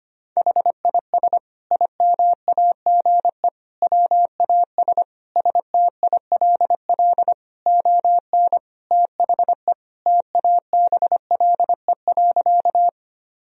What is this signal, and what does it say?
Morse code, heard as a keyed tone, about 25 wpm, 700 Hz. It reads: HIS IMAGE WAS STILL ON THE TABLE.
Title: Sentences from Top 1000 Words - Review Part 4 - 25wpm